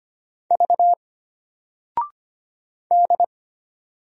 Morse code, heard as a keyed tone, about 25 wpm, 700 Hz. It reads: V D
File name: Single-Letter-Number-25wpm